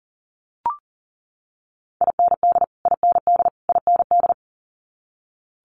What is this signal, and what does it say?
Morse code, heard as a keyed tone, about 39 wpm, 690 Hz. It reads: IND IND IND
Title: Multi-letter Phonemes Repeated 3x 40wpm